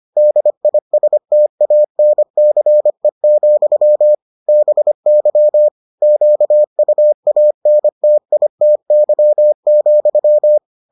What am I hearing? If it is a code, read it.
Morse code, heard as a keyed tone, about 25 wpm, 600 Hz. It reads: DISTANCE, BY QUANTITY,